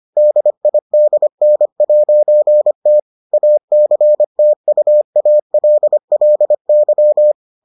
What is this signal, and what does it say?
Morse code, heard as a keyed tone, about 25 wpm, 600 Hz. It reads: DIDN'T ACTUALLY